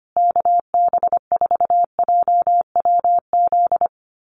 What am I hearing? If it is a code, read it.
Morse code, heard as a keyed tone, about 25 wpm, 700 Hz. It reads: KB4JWZ
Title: USA Call Signs Repeated 2x 25wpm